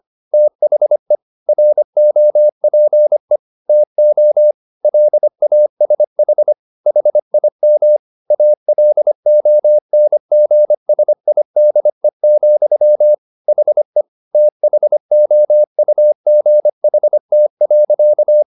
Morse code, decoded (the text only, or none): THE ROPE TO LASH HIM ALONGSIDE, HE THOUGHT.